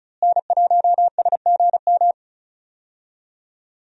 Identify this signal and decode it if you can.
Morse code, heard as a keyed tone, about 36 wpm, 700 Hz. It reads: N1SGM
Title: USA Call Signs - 5 Characters - Repeated 2x - 35wpm